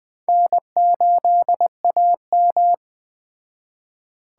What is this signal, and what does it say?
Morse code, heard as a keyed tone, about 20 wpm, 700 Hz. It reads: N8AM
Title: Speed Racing - USA 1x2 Call Signs 17wpm